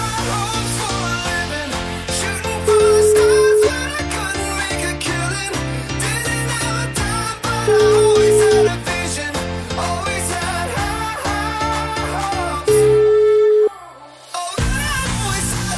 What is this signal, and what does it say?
Ringback tone (Europe, 425 Hz), with noise.